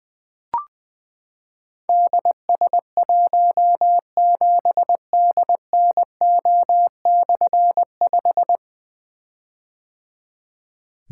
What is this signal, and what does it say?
Morse code, heard as a keyed tone, about 20 wpm, 700 Hz. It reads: DS17DNO/5
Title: Difficult Call Signs 20wpm